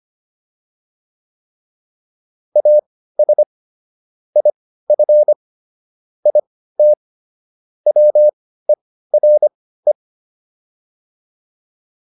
Morse code, heard as a keyed tone, about 25 wpm, spaced as for 15 wpm, 600 Hz. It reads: AS IF IT WERE